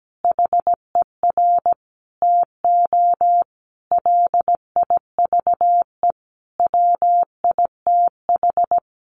Morse code, heard as a keyed tone, about 17 wpm, 700 Hz. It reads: HER TO LIVE WITH
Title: Sentences from Top 1100 Words 17wpm